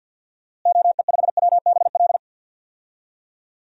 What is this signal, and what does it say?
Morse code, heard as a keyed tone, about 48 wpm, 700 Hz. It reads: OE5WBL
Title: DXCC-100 Call Signs - 6 Characters - Repeated 2x - 50wpm